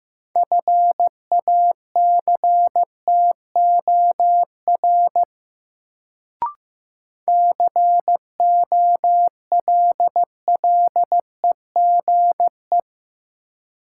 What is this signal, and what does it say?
Morse code, heard as a keyed tone, about 15 wpm, 700 Hz. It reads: FACTOR COLLEGE
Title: Sentences from Top 1300 Words - New Word Drill Alt - 15wpm